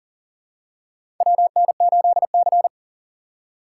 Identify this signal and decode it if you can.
Morse code, heard as a keyed tone, about 40 wpm, 700 Hz. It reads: WN8C
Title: USA Call Signs - 4 Characters - 2x Character Spacing - 40wpm